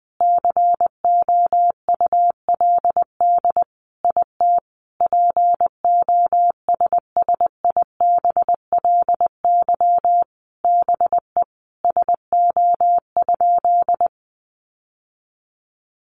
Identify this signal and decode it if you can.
Morse code, heard as a keyed tone, about 20 wpm, 700 Hz. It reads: COULD IT POSSIBLY BE SO?
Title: Sentences from Top 1300 Words 20wpm